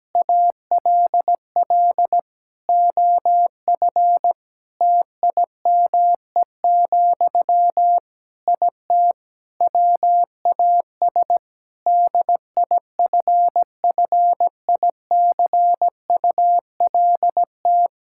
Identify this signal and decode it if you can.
Morse code, heard as a keyed tone, about 17 wpm, 700 Hz. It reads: ALL OF TIME, IT WAS DIFFICULT
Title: Sentences from Top 1000 Words - Review Part 5 - 17wpm